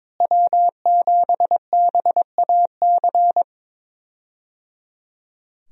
Morse code, heard as a keyed tone, about 22 wpm, 700 Hz. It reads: W7BAC
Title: USA Call Signs - 5 Characters - Repeated 3x - 22wpm